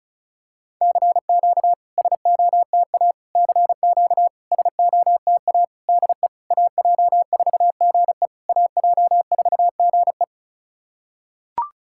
Morse code, heard as a keyed tone, about 36 wpm, 690 Hz. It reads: CQ SOTA CQ SOTA DE AJ4GE AJ4GE
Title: SOTA QSO Practice 35wpm